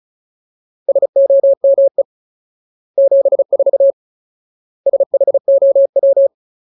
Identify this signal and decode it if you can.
Morse code, heard as a keyed tone, about 34 wpm, 550 Hz. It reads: SOME 74 SHOW
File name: Ultimate Word List / 35WPM Ultimate